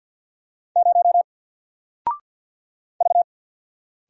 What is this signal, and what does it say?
Morse code, heard as a keyed tone, about 48 wpm, 700 Hz. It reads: 0 V
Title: Single Character (Letter:Number:Punctuation:Prosign) Alt 50wpm